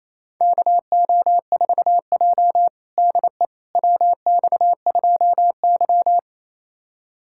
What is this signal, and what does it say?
Morse code, heard as a keyed tone, about 28 wpm, 700 Hz. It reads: KO4J DE WX2Y
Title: USA Callsign DE Callsign - 4 Characters - 28wpm